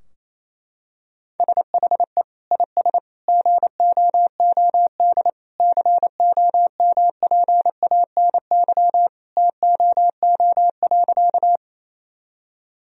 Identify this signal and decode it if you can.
Morse code, heard as a keyed tone, about 28 wpm, 700 Hz. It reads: SHE IS GOOD COMPANY TOO.
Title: Sentences from Top 1000 Words - Review Part 3 - 28wpm